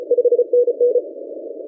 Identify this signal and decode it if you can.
Morse code, heard as a keyed tone, about 32 wpm, 490 Hz. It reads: 5NN
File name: PodcastGenerator / k1n-navassa-40cw- -0247z1